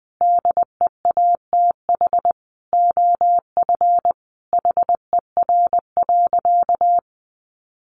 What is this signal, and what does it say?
Morse code, heard as a keyed tone, about 20 wpm, 700 Hz. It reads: DEATH OF HER.
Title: Sentences from Top 1100 Words 20wpm